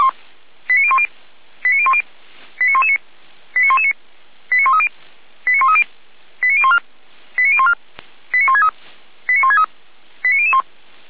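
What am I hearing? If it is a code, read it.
Five-tone selective call (ZVEI), heard as quick runs of five tones, tones of about 70 ms. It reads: 89010 89010 89119 89119 89130 89130 89014 89014 89163 89163 89001
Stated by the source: ZVEI five-tone call received on SDR.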